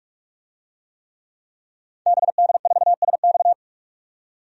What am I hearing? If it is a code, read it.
Morse code, heard as a keyed tone, about 46 wpm, 690 Hz. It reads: BD4SX
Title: DXCC-100 Call Signs - 5 Characters - 45wpm